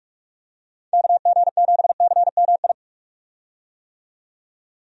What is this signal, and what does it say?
Morse code, heard as a keyed tone, about 46 wpm, 700 Hz. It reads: KG7CMI
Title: USA Call Signs - 6 Characters - 45wpm